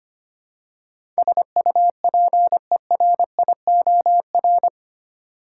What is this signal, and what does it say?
Morse code, heard as a keyed tone, about 25 wpm, 700 Hz. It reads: SUPERIOR
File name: 8 Letter Words - First 3 Letters Given - 25wpm